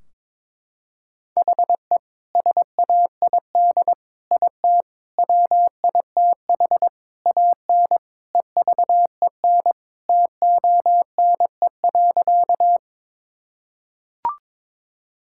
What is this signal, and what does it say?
Morse code, heard as a keyed tone, about 22 wpm, 700 Hz. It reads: HE SAID IT WITH AN EVEN TONE.
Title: Sentences from Top 1600 Words 22wpm